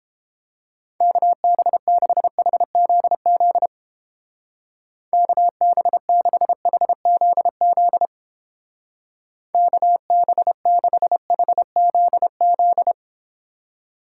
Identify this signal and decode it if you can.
Morse code, heard as a keyed tone, about 29 wpm, 700 Hz. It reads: KB6HZZ KB6HZZ KB6HZZ